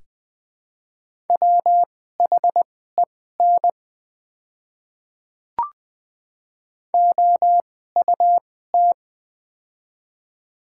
Morse code, heard as a keyed tone, about 20 wpm, spaced as for 14 wpm, 700 Hz. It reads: WHEN OUT
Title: 2x Character Spacing - Top 100 Words - 20wpm